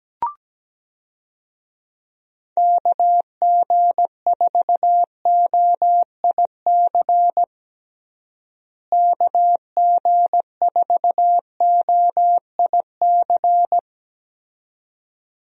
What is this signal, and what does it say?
Morse code, heard as a keyed tone, about 17 wpm, 700 Hz. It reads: KG4OIC KG4OIC
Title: USA Call Signs - 6 Characters - Repeated 2x - 17wpm